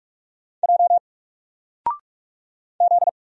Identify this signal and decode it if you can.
Morse code, heard as a keyed tone, about 45 wpm, 700 Hz. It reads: J Z
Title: Single Letter-Number Alt 45wpm